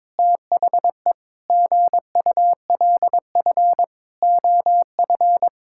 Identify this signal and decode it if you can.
Morse code, heard as a keyed tone, about 22 wpm, 700 Hz. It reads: THE GULF OF
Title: Pangram Challenge v2 - 22wpm